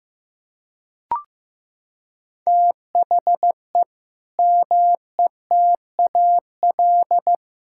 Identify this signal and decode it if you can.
Morse code, heard as a keyed tone, about 15 wpm, 700 Hz. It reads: THE METAL
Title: Sentences from Top 1000 Words - Review Part 6 - 15wpm